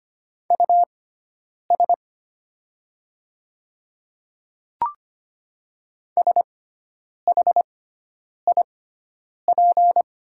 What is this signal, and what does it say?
Morse code, heard as a keyed tone, about 25 wpm, spaced as for 9 wpm, 690 Hz. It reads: US SHIP
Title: 6x Character Spacing - Top 500 Words - 25wpm